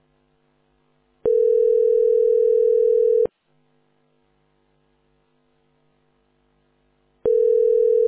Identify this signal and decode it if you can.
Ringback tone (North America), 440 + 480 Hz.